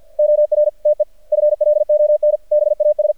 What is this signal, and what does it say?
Morse code, heard as a keyed tone, about 51 wpm, spaced as for 45 wpm, 600 Hz, with radio noise. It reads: OR TEFLON BAS
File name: archived / CRSnet